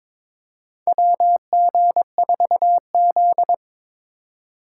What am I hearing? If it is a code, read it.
Morse code, heard as a keyed tone, about 22 wpm, 700 Hz. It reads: WG4Z
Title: USA Call Signs - 4 Characters - 2x Character Spacing - 22wpm